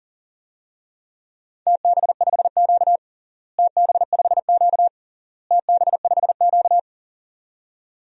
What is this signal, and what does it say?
Morse code, heard as a keyed tone, about 40 wpm, 700 Hz. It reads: TB5Q TB5Q TB5Q
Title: DXCC-100 Call Signs - 4 Characters - Repeated 3x - 40wpm